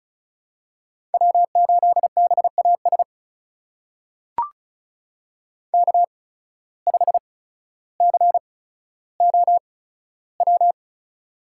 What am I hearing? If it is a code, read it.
Morse code, heard as a keyed tone, about 36 wpm, spaced as for 15 wpm, 700 Hz. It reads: W8BAS K5COW
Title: USA Call Signs - 5 Characters - 8x Character Spacing - 35wpm